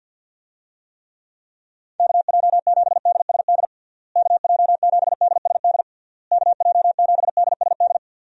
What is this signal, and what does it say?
Morse code, heard as a keyed tone, about 52 wpm, 700 Hz. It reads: KJ7DSD KJ7DSD KJ7DSD
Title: USA Call Signs - 6 Characters - Repeated 3x - 50wpm